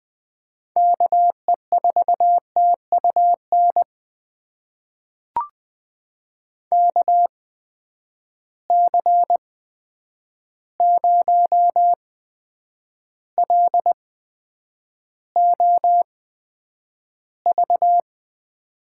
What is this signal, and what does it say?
Morse code, heard as a keyed tone, about 20 wpm, spaced as for 9 wpm, 700 Hz. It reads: KE4TUN KC0LOV
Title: USA Call Signs - 6 Characters - 8x Character Spacing - 20wpm